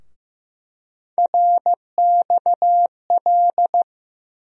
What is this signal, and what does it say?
Morse code, heard as a keyed tone, about 15 wpm, 700 Hz. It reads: RXL
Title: USA Call Sign Postfixes - 3 Letters 15wpm